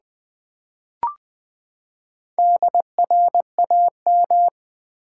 Morse code, heard as a keyed tone, about 20 wpm, 700 Hz. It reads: DRAM
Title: Build A Word - 4 Letters At A Time - 12-Letter Words - 20wpm